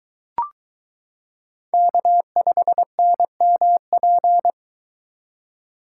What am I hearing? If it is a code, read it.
Morse code, heard as a keyed tone, about 22 wpm, 700 Hz. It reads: K5NMP